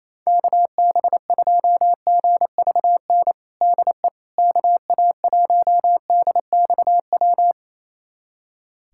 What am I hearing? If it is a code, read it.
Morse code, heard as a keyed tone, about 28 wpm, 700 Hz. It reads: KB2GVN DE KA1DXW
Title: USA Callsign DE Callsign - 6 Characters - 28wpm